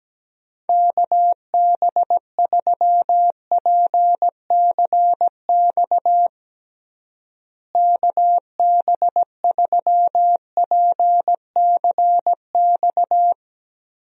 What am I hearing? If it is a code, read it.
Morse code, heard as a keyed tone, about 17 wpm, 700 Hz. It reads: KB3PCX KB3PCX
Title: USA Call Signs - 6 Characters - Repeated 3x - 17wpm